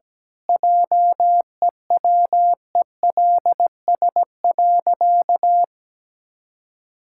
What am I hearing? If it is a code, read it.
Morse code, heard as a keyed tone, about 17 wpm, 700 Hz. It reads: JEWELS.
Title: Pangram Challenge v2 - 17wpm